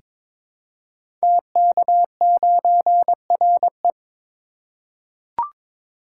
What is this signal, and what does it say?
Morse code, heard as a keyed tone, about 22 wpm, 700 Hz. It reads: TK9RE
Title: Code Groups - 5 Letter:Numbers - 22wpm